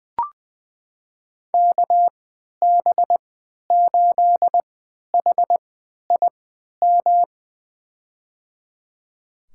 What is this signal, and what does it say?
Morse code, heard as a keyed tone, about 20 wpm, spaced as for 11 wpm, 700 Hz. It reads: KB8HIM
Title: USA Call Signs - 6 Characters - 3x Character Spacing - 20wpm